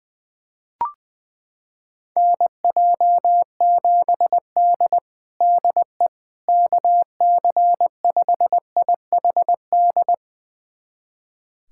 Morse code, heard as a keyed tone, about 20 wpm, 700 Hz. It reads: NJ7D DE KC5IHD
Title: USA Callsign DE Callsign 20wpm